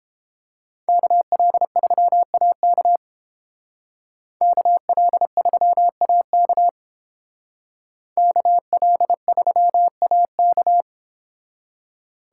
Morse code, heard as a keyed tone, about 30 wpm, 700 Hz. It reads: KL3AK KL3AK KL3AK